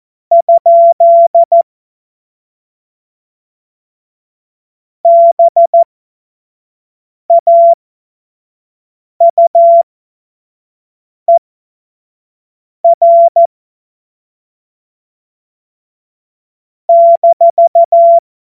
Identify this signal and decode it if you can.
Morse code, heard as a keyed tone, about 14 wpm, spaced as for 5 wpm, 680 Hz. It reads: ? BAUER -
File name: Witz-09-September-2025-25-F70